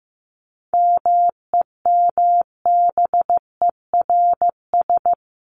Morse code, heard as a keyed tone, about 15 wpm, 700 Hz. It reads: MEMBERS
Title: ES:S Suffix - Top 100 Words with Base Words 15wpm